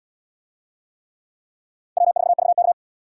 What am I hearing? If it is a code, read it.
Morse code, heard as a keyed tone, about 74 wpm, 700 Hz. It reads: W4VK